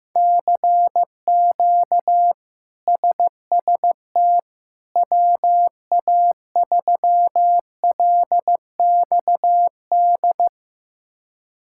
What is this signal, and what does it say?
Morse code, heard as a keyed tone, about 16 wpm, 700 Hz. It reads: CQ SST WA3LXD
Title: K1USN SST QSO Practice - 15wpm